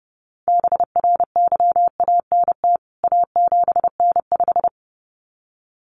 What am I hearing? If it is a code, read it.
Morse code, heard as a keyed tone, about 30 wpm, 700 Hz. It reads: BRYANT A7N5
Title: CWT Exchange 30wpm